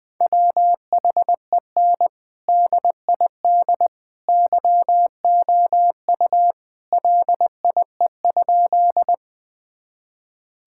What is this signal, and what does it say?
Morse code, heard as a keyed tone, about 20 wpm, 700 Hz. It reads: WHEN DID YOU LIE?
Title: Sentences from Top 1600 Words 20wpm